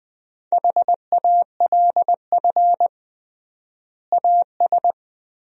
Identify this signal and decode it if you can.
Morse code, heard as a keyed tone, about 20 wpm, 700 Hz. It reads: HALF AS
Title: 3x Word-Spacing - Sentences from Top 400 Words - 20wpm